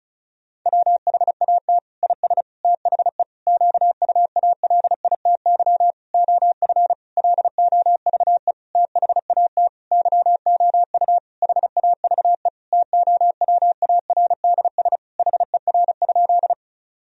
Morse code, heard as a keyed tone, about 36 wpm, 700 Hz. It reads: WHAT IS THE QUALITY OF LOVE THAT YOU HAVE TOWARDS HER?
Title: Sentences from Top 1000 Words - Review Part 4 - 35wpm